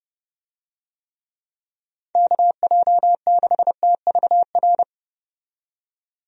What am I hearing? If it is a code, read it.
Morse code, heard as a keyed tone, about 30 wpm, 700 Hz. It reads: KJ6TVR
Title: USA Call Signs - 6 Characters - 30wpm